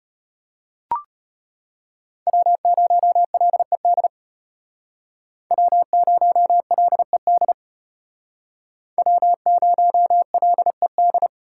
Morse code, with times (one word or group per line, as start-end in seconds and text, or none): 2.3-4.1 W0LED
5.5-7.5 W0LED
9.0-11.3 W0LED